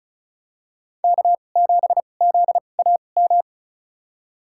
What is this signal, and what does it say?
Morse code, heard as a keyed tone, about 35 wpm, spaced as for 25 wpm, 690 Hz. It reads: K7ZAM
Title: USA Call Signs - 5 Characters - 2x Character Spacing - 35wpm